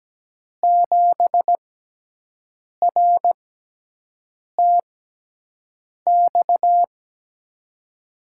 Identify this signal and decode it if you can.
Morse code, heard as a keyed tone, about 17 wpm, spaced as for 6 wpm, 700 Hz. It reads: 7RTX
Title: USA Call Signs - 5 Characters - 6x Character Spacing - 17wpm